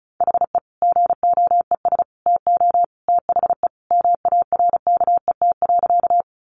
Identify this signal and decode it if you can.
Morse code, heard as a keyed tone, about 34 wpm, 700 Hz. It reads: HE GOES TO THE MARKET.
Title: Additional Easy Phrases 35wpm